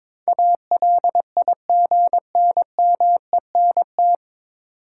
Morse code, heard as a keyed tone, about 21 wpm, 700 Hz. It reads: ALIGNMENT
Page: Morse Code Ninja practice 22wpm